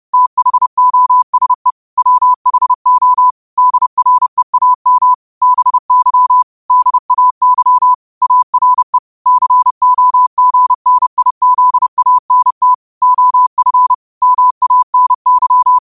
Morse code, heard as a keyed tone, about 30 wpm, 1000 Hz. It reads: THOSE WHO DREAM BY DAY ARE COGNIZANT OF MANY
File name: qotdfiles / 30wpm